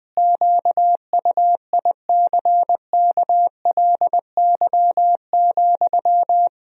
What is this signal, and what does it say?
Morse code, heard as a keyed tone, about 20 wpm, 700 Hz. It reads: QUICKLY,
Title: Sentences from Top 1000 Words - Review Part 4 - 20wpm